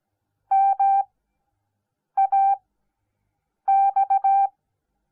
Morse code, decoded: MAX

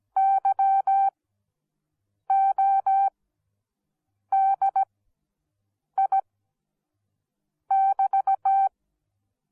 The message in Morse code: YODI<BT>

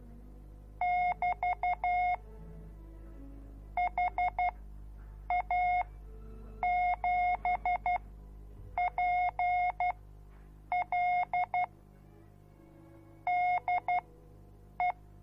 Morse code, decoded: <BT> HA7PL DE

